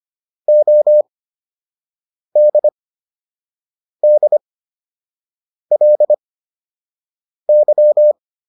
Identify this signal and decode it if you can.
Morse code, heard as a keyed tone, about 25 wpm, spaced as for 6 wpm, 600 Hz. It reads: ODDLY